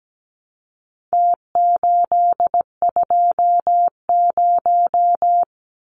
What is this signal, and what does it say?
Morse code, heard as a keyed tone, about 17 wpm, 700 Hz. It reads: T820
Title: Numbers - Contest Style - 4-Digit 17wpm